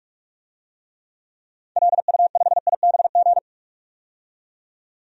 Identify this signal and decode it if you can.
Morse code, heard as a keyed tone, about 46 wpm, 700 Hz. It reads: LU5IBG
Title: DXCC-100 Call Signs - 6 Characters - 45wpm